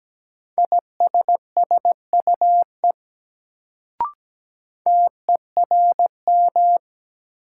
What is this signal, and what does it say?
Morse code, heard as a keyed tone, about 17 wpm, 700 Hz. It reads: ISSUE TERM
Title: Sentences from Top 900 Words - New Word Drill - 17wpm